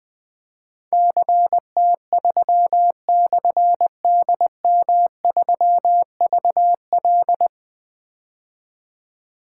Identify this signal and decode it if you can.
Morse code, heard as a keyed tone, about 20 wpm, 700 Hz. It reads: CT3/DM3VL
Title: Difficult Call Signs 20wpm